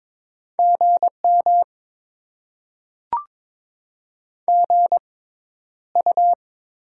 Morse code, heard as a keyed tone, about 21 wpm, spaced as for 11 wpm, 700 Hz. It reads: GM GU
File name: Top 20 QSO Elements - 6x Character Spacing - 22wpm